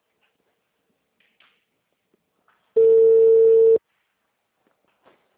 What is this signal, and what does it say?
Ringback tone (China, 450 Hz).